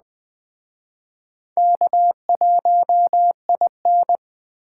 Morse code, heard as a keyed tone, about 20 wpm, 700 Hz. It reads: K1IN
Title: Speed Racing - USA 1x2 Call Signs 17wpm